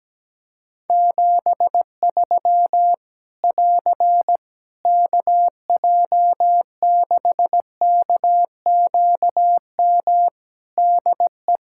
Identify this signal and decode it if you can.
Morse code, heard as a keyed tone, about 17 wpm, 700 Hz. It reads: 73 <AR> KJ6KQM DE
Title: K1USN SST QSO Practice 17wpm